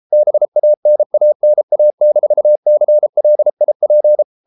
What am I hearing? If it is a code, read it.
Morse code, heard as a keyed tone, about 33 wpm, 600 Hz. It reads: BANANA-CLIP